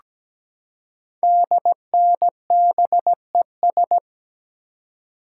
Morse code, heard as a keyed tone, about 17 wpm, 700 Hz. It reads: DNBES
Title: Code Groups - 5 Letters - 17wpm